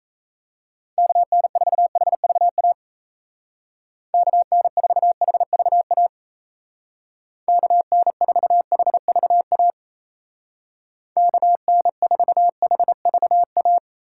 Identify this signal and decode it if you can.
Morse code, heard as a keyed tone, about 34 wpm, 700 Hz. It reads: KN4HVA KN4HVA KN4HVA KN4HVA
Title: Speed Racing - USA 2x3 Call Signs 28wpm